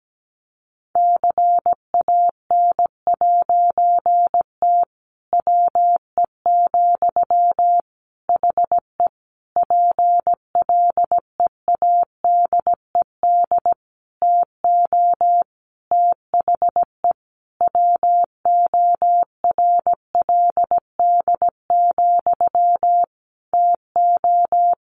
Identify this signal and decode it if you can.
Morse code, heard as a keyed tone, about 17 wpm, 700 Hz. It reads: CAN'T WE, HE PLEADED TO THE WORLD, TO
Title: A Sound Of Thunder 17wpm